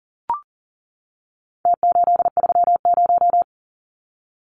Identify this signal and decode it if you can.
Morse code, heard as a keyed tone, about 40 wpm, 700 Hz. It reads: T830
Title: Numbers - Contest Style - 4-Digit 40wpm